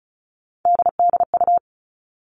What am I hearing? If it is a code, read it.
Morse code, heard as a keyed tone, about 34 wpm, 700 Hz. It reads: DDU
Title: All Cut Numbers - 3-Digit 35wpm